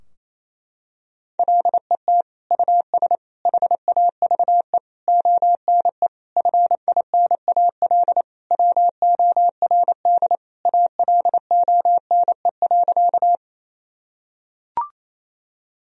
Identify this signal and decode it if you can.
Morse code, heard as a keyed tone, about 28 wpm, 700 Hz. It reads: LET US HAVE ONE FINAL WORD ALONE.
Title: Sentences from Top 1000 Words - Review Part 2 - 28wpm